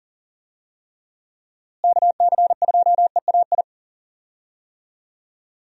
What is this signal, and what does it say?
Morse code, heard as a keyed tone, about 40 wpm, 700 Hz. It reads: KC2EAI
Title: USA Call Signs - 6 Characters - Repeated 2x - 40wpm